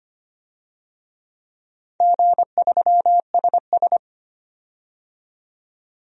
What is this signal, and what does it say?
Morse code, heard as a keyed tone, about 25 wpm, 700 Hz. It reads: G3SS
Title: DXCC-100 Call Signs - 4 Characters - 25wpm